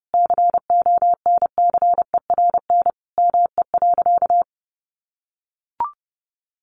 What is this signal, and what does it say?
Morse code, heard as a keyed tone, about 30 wpm, 700 Hz. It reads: CONCERN ME.
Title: Sentences from Top 1000 Words 30wpm